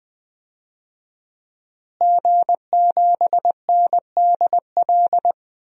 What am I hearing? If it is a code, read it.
Morse code, heard as a keyed tone, about 20 wpm, 700 Hz. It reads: G7NDL
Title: DXCC-100 Call Signs - 5 Characters - Repeated 3x - 20wpm